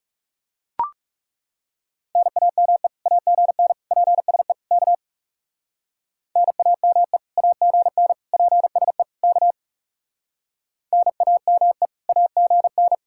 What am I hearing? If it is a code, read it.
Morse code, heard as a keyed tone, about 38 wpm, 700 Hz. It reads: NAME AGN PSE K NAME AGN PSE K NAME AGN